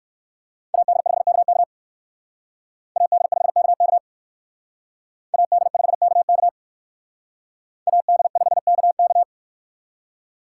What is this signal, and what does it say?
Morse code, heard as a keyed tone, about 58 wpm, 700 Hz. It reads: AD5KK AD5KK AD5KK AD5KK